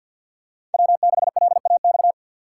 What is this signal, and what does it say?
Morse code, heard as a keyed tone, about 51 wpm, 700 Hz. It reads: W6LAX
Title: K1USN SST QSO Practice - 50wpm